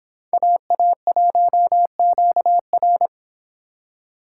Morse code, heard as a keyed tone, about 25 wpm, 700 Hz. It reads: AA1QR